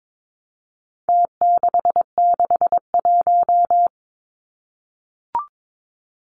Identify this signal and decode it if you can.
Morse code, heard as a keyed tone, about 22 wpm, 700 Hz. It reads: T661
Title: Numbers - Contest Style - 4-Digit 22wpm